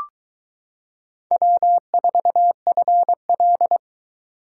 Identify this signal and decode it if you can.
Morse code, heard as a keyed tone, about 22 wpm, 700 Hz. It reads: W4FL